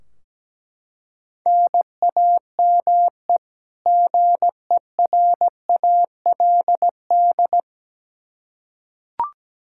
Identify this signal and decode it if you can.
Morse code, heard as a keyed tone, about 17 wpm, 700 Hz. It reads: NAME GERALD
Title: Practice QSO - Exchange 1 - Beginning - 17wpm